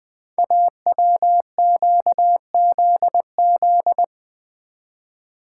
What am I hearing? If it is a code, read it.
Morse code, heard as a keyed tone, about 20 wpm, 700 Hz. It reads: AWQZZ
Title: Code Groups - 5 Letters - 20wpm